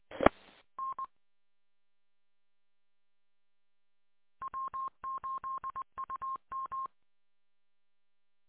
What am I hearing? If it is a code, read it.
Morse code, heard as a keyed tone, about 20 wpm, 1060 Hz. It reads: E N W8UM